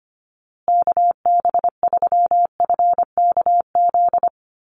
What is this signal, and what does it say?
Morse code, heard as a keyed tone, about 25 wpm, 700 Hz. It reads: KB3FKZ
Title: USA Call Signs Repeated 3x 25wpm